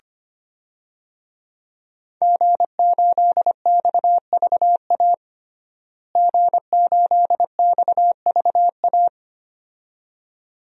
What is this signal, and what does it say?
Morse code, heard as a keyed tone, about 25 wpm, 700 Hz. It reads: G8XVA G8XVA
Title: DXCC-100 Call Signs - 5 Characters - Repeated 2x - 25wpm